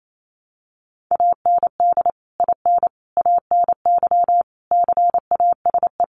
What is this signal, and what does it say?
Morse code, heard as a keyed tone, about 28 wpm, 700 Hz. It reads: AND IN ANY CASE
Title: Sentences from Top 1200 Words 28wpm